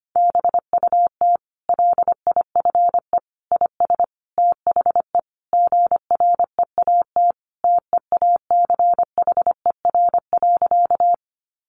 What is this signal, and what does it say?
Morse code, heard as a keyed tone, about 25 wpm, 700 Hz. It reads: BUT LIFE IS THE GREAT TEACHER.
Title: Sentences from Top 900 Words 25wpm